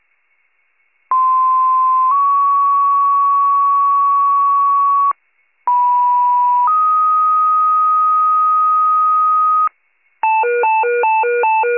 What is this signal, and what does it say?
Two-tone page, heard as a one-second tone, then a three-second tone, about 1040 then 1120 Hz, about 980 then 1250 Hz. It is heard from 1.1 to 9.7 s.